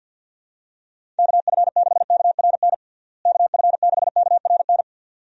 Morse code, heard as a keyed tone, about 51 wpm, 700 Hz. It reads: KF6KRN KF6KRN
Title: USA Call Signs - 6 Characters - Repeated 2x - 50wpm